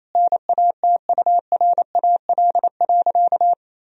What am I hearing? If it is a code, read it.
Morse code, heard as a keyed tone, about 28 wpm, 700 Hz. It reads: NATURAL.
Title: Sentences from Top 1000 Words - Review Part 4 - 28wpm